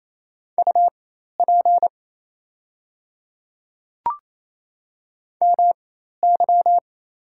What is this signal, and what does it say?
Morse code, heard as a keyed tone, about 28 wpm, spaced as for 13 wpm, 700 Hz. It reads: UP MY
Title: 4x Character Spacing - Top 300 Words - 28wpm